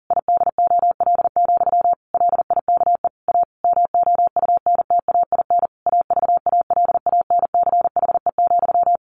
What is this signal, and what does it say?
Morse code, heard as a keyed tone, about 40 wpm, 700 Hz. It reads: IDOL, LIKE A MOUNTAIN AVALANCHE,
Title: A Sound Of Thunder 40wpm